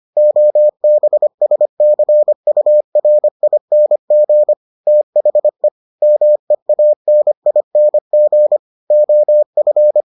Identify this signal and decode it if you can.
Morse code, heard as a keyed tone, about 25 wpm, 600 Hz. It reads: OBSCURING THE MEANING OF